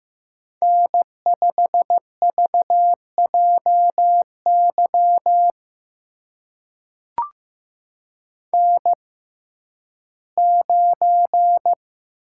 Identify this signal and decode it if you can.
Morse code, heard as a keyed tone, about 15 wpm, spaced as for 11 wpm, 700 Hz. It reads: N5VJY N9
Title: USA Call Signs - 5 Characters - 6x Character Spacing - 15wpm